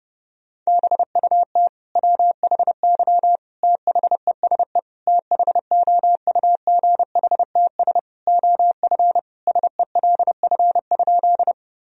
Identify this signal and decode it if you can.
Morse code, heard as a keyed tone, about 30 wpm, 700 Hz. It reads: BUT WHY THESE THOUGHTS OF SELF?
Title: Sentences from Top 1000 Words - Review Part 1 - 30wpm